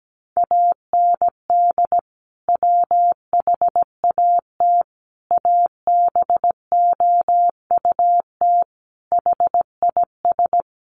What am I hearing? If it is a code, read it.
Morse code, heard as a keyed tone, about 17 wpm, 700 Hz. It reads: AND WHAT ABOUT HIS
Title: Sentences from Top 1200 Words 17wpm